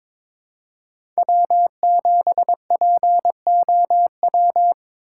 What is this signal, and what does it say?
Morse code, heard as a keyed tone, about 22 wpm, 700 Hz. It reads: W7POW
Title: USA Call Signs - 5 Characters - 6x Character Spacing - 22wpm